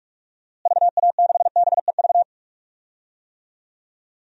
Morse code, heard as a keyed tone, about 46 wpm, 700 Hz. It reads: VA6BEV
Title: DXCC-100 Call Signs - 6 Characters - Repeated 3x - 45wpm